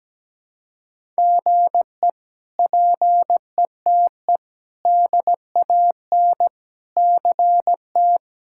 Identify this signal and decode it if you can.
Morse code, heard as a keyed tone, about 17 wpm, 700 Hz. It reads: GE PETE DAN CT
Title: K1USN SST QSO Practice - 17wpm